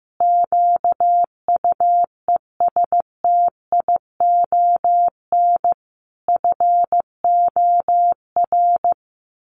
Morse code, heard as a keyed tone, about 15 wpm, 700 Hz. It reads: QUESTION FOR
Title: Sets of 5 Words Encore2 15wpm